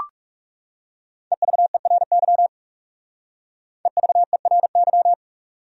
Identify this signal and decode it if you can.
Morse code, heard as a keyed tone, about 37 wpm, 700 Hz. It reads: EVERY EVERY
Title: Speed-Racing Top 200 Words 30wpm